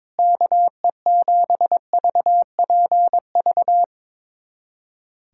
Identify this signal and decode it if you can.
Morse code, heard as a keyed tone, about 22 wpm, 700 Hz. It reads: KE7VPV
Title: USA Callsign DE Callsign - 6 Characters - 22wpm